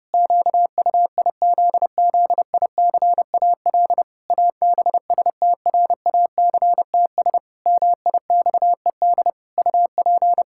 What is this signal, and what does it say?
Morse code, heard as a keyed tone, about 30 wpm, 700 Hz. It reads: QUIZZICAL ABSTRACTS MIXED UP
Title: Pangram Challenge v2 - 30wpm